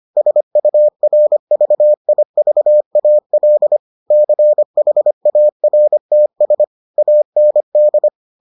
Morse code, heard as a keyed tone, about 25 wpm, 600 Hz. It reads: SURVIVAL CHARTS AND